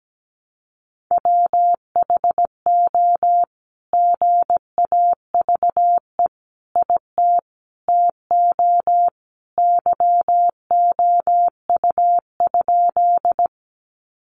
Morse code, heard as a keyed tone, about 17 wpm, 700 Hz. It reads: WHO GAVE IT TO YOU?
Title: Sentences from Top 500 Words 17wpm